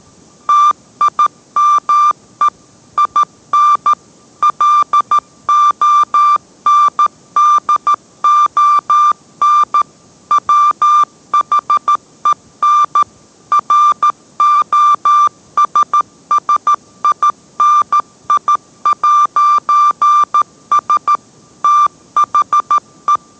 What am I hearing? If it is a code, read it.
Morse code, heard as a keyed tone, about 13 wpm, 1180 Hz. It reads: TIME IN LONDON WHEN ROSSINI'S THE